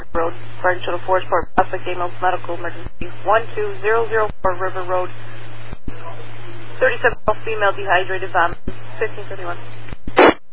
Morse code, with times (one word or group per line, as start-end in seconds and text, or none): none